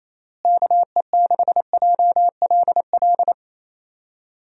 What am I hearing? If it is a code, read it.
Morse code, heard as a keyed tone, about 28 wpm, 700 Hz. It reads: KE6JLL
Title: USA Call Signs - 6 Characters - Repeated 3x - 28wpm